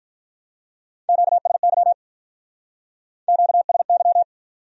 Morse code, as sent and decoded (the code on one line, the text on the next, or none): --.- ... -.-- / --.- ... -.--
QSY QSY